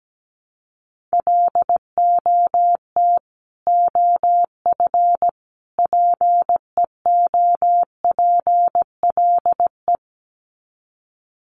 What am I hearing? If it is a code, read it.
Morse code, heard as a keyed tone, about 17 wpm, 700 Hz. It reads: LOT OF PEOPLE
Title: Sets of 3 Words 17wpm